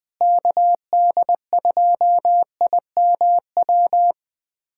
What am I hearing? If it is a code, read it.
Morse code, heard as a keyed tone, about 19 wpm, 700 Hz. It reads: KD2IMW
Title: USA Callsign DE Callsign 20wpm